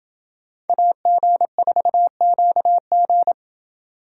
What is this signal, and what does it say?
Morse code, heard as a keyed tone, about 28 wpm, 700 Hz. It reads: AG4QG